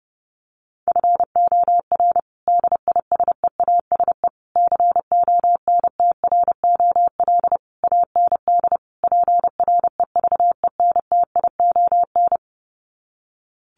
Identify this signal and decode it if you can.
Morse code, heard as a keyed tone, about 28 wpm, 700 Hz. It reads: FOR DISEASE CONTROL AND PREVENTION
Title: Sets of 5 Words Encore2 30wpm